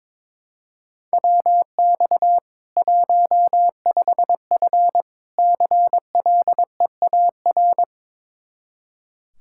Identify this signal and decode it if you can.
Morse code, heard as a keyed tone, about 22 wpm, 700 Hz. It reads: WX 15F CLEAR
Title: Practice QSO - Exchange 2 - Info - 22wpm